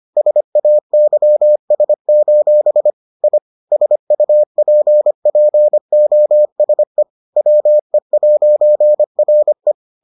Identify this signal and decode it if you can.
Morse code, heard as a keyed tone, about 25 wpm, 600 Hz. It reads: SAYS: I SUPPOSE WE'RE